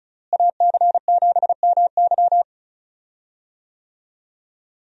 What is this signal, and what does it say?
Morse code, heard as a keyed tone, about 34 wpm, 700 Hz. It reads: AC7MY
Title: USA Call Signs - 5 Characters - Repeated 2x - 35wpm